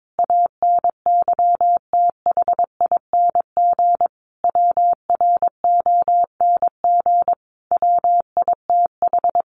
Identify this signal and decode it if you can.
Morse code, heard as a keyed tone, about 22 wpm, 700 Hz. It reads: ANYTHING WRONG WITH
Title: Sentences from Top 1500 Words 22wpm